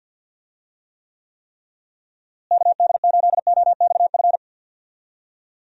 Morse code, heard as a keyed tone, about 48 wpm, 700 Hz. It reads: KD8OXF
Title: USA Call Signs - 6 Characters - Repeated 3x - 50wpm